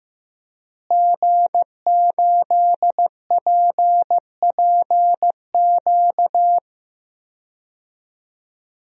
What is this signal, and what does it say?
Morse code, heard as a keyed tone, about 15 wpm, 700 Hz. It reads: G8PPQ
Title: DXCC-100 Call Signs - 5 Characters - Repeated 3x - 15wpm